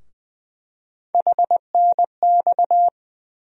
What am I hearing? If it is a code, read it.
Morse code, heard as a keyed tone, about 20 wpm, 700 Hz. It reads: HNX
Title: USA Call Sign Postfixes - 3 Letters 20wpm